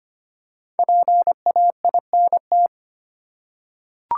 Morse code, heard as a keyed tone, about 25 wpm, 700 Hz. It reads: PAINT
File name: practice / 25wpm / Easy Words with Hints - 25wpm